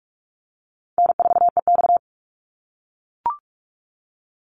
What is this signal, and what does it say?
Morse code, heard as a keyed tone, about 44 wpm, 700 Hz. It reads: N4EX
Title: NA SOTA Chaser Callsigns 45wpm